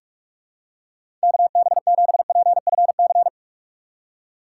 Morse code, heard as a keyed tone, about 42 wpm, 700 Hz. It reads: KB7PFC
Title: USA Call Signs - 6 Characters - 3x Character Spacing - 45wpm